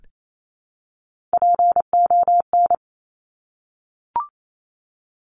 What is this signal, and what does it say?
Morse code, heard as a keyed tone, about 28 wpm, 700 Hz. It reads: PON
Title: Top Three Letter Combinations in Words 28wpm